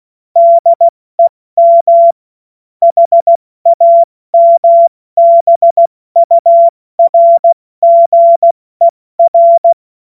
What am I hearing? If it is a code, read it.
Morse code, heard as a keyed tone, about 16 wpm, 680 Hz. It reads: DEM HAMBURGER
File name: Morsekurs / Witze / Witz 08. August 2023 70 F80